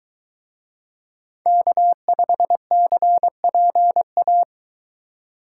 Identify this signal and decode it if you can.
Morse code, heard as a keyed tone, about 23 wpm, 700 Hz. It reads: K5CPA